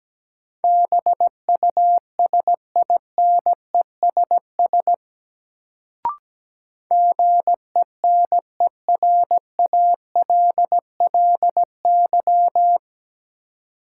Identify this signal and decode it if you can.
Morse code, heard as a keyed tone, about 17 wpm, 700 Hz. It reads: BUSINESS GENERALLY
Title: Sentences from Top 600 Words - New Word Drill - 17wpm